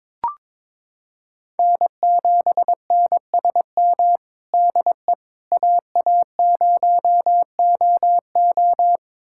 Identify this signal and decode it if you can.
Morse code, heard as a keyed tone, about 21 wpm, 700 Hz. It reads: N7NSM DE AA0OO
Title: USA Callsign DE Callsign - 5 Characters - 22wpm